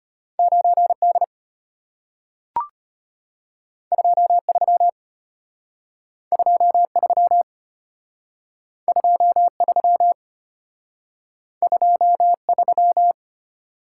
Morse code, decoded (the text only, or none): 9D 23 23 23 23